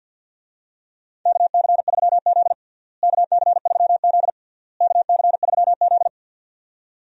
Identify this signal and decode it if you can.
Morse code, heard as a keyed tone, about 50 wpm, 700 Hz. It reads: KC3Z KC3Z KC3Z
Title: USA Call Signs - 4 Characters - Repeated 3x - 50wpm